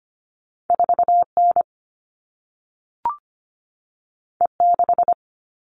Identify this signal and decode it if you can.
Morse code, heard as a keyed tone, about 25 wpm, 700 Hz. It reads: 4N E6
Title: All Cut Numbers - 2-Digit 25wpm